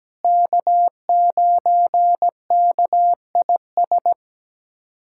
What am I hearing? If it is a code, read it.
Morse code, heard as a keyed tone, about 17 wpm, 700 Hz. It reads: K9KIS
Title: USA Callsign DE Callsign - 5 Characters - 17wpm